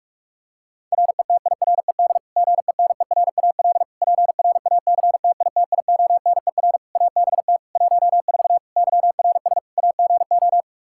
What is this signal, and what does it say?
Morse code, heard as a keyed tone, about 48 wpm, 700 Hz. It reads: RETIRED GENERAL PRACTITIONER ABT 14 YRS AGO
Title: Practice QSO - Exchange 3 - More Info - 45wpm